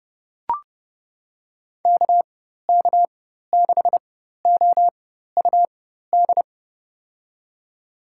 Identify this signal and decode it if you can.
Morse code, heard as a keyed tone, about 30 wpm, spaced as for 14 wpm, 700 Hz. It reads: KK6OUD
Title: USA Call Signs - 6 Characters - 4x Character Spacing - 30wpm